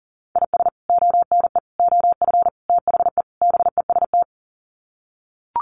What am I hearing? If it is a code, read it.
Morse code, heard as a keyed tone, about 38 wpm, 700 Hz. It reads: IS ONE OF THE BEST
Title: Sets of 5 Words 40wpm